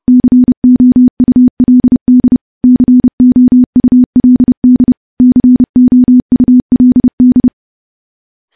Morse code, heard as a keyed tone, about 28 wpm, 250 Hz. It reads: COULD COULD COULD
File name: morse / morse-training-30WPM